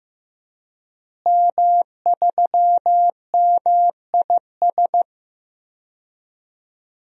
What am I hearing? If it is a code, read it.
Morse code, heard as a keyed tone, about 15 wpm, 700 Hz. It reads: M3MIS
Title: DXCC-100 Call Signs - 5 Characters - Repeated 2x - 15wpm